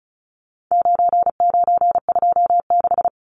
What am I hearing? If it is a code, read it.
Morse code, heard as a keyed tone, about 34 wpm, 700 Hz. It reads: 9926
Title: Numbers - 4-Digit 35wpm